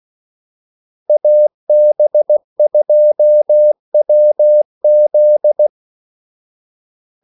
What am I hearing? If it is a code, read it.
Morse code, heard as a keyed tone, about 16 wpm, 600 Hz. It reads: AB2WZ
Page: Kaldesignaler 16 wpm